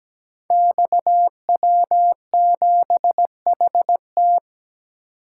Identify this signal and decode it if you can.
Morse code, heard as a keyed tone, about 17 wpm, 700 Hz. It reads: XW7HT
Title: Code Groups - 5 Letter:Numbers - 17wpm